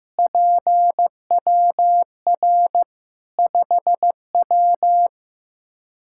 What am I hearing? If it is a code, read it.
Morse code, heard as a keyed tone, about 15 wpm, 700 Hz. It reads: PWR 5W
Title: Practice QSO - Exchange 2 - Info - 15wpm